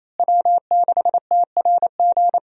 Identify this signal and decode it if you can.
Morse code, heard as a keyed tone, about 28 wpm, 700 Hz. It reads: W6TRG
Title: USA Call Signs - 5 Characters - Repeated 2x - 28wpm